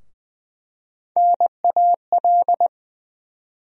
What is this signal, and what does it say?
Morse code, heard as a keyed tone, about 20 wpm, 700 Hz. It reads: NAL
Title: USA Call Sign Postfixes - 3 Letters 20wpm